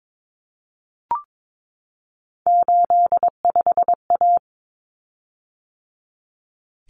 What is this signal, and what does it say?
Morse code, heard as a keyed tone, about 22 wpm, 700 Hz. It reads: 85A
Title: Common Cut Numbers - 3-Digit 22wpm